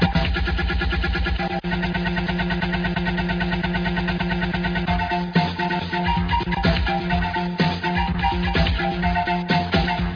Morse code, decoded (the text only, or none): none